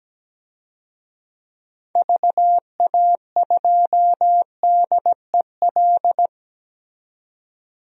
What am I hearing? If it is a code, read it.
Morse code, heard as a keyed tone, about 17 wpm, 700 Hz. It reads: VA2DEL
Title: DXCC-100 Call Signs - 6 Characters - 17wpm